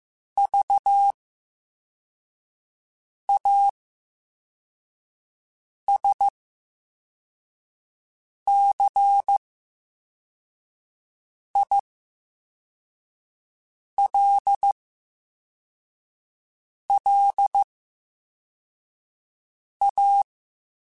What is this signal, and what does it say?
Morse code, heard as a keyed tone, about 15 wpm, 800 Hz. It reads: V A S C I L L A
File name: Special INGs-mp3s / Special INGs-mp3s / VerySlowToFaster Ing @13